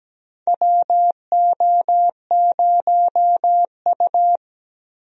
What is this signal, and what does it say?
Morse code, heard as a keyed tone, about 17 wpm, 700 Hz. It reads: WO0U
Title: USA Call Signs - 4 Characters - Repeated 3x - 17wpm